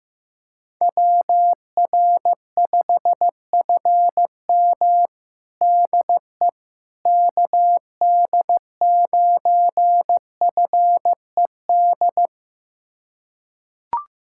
Morse code, heard as a keyed tone, about 15 wpm, 700 Hz. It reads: WR5FM DE KD9FED
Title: USA Callsign DE Callsign 15wpm